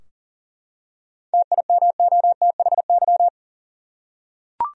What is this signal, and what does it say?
Morse code, heard as a keyed tone, about 40 wpm, 700 Hz. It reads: TIMOTHY
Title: USA Common Names - 40wpm